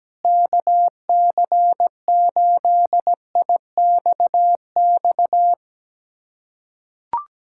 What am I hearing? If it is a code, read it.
Morse code, heard as a keyed tone, about 17 wpm, 700 Hz. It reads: KC8IXX
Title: USA Callsign DE Callsign 17wpm